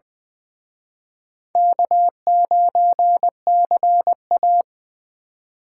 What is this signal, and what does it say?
Morse code, heard as a keyed tone, about 24 wpm, 700 Hz. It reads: K9CA
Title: Speed Racing - USA Call Signs 20wpm